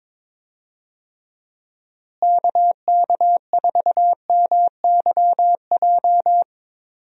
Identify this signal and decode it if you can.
Morse code, heard as a keyed tone, about 22 wpm, 700 Hz. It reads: KK4MYJ
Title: USA Call Signs - 6 Characters - Repeated 2x - 22wpm